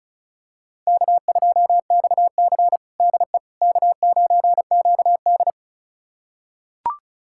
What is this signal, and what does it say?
Morse code, heard as a keyed tone, about 36 wpm, 700 Hz. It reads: K2XC DE K9QD
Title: USA Callsign DE Callsign - 4 Characters - 35wpm